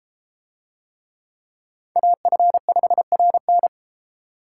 Morse code, heard as a keyed tone, about 32 wpm, 700 Hz. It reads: AF5RN